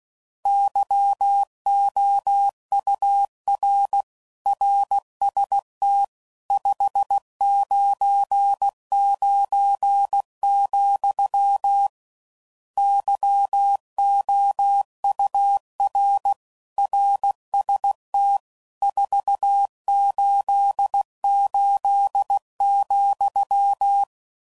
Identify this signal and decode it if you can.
Morse code, heard as a keyed tone, about 16 wpm, 800 Hz. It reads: YOUR RST 599, YOUR RST 488,
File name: CW Code Groups @15WPM / RSTs / Your RSTs